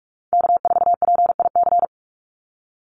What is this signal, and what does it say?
Morse code, heard as a keyed tone, about 46 wpm, 690 Hz. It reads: K4PIC
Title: NA SOTA Chaser Callsigns 45wpm